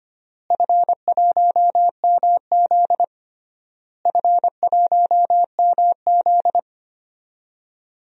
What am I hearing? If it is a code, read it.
Morse code, heard as a keyed tone, about 25 wpm, 700 Hz. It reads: F1MZ F1MZ
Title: DXCC-100 Call Signs - 4 Characters - Repeated 3x - 25wpm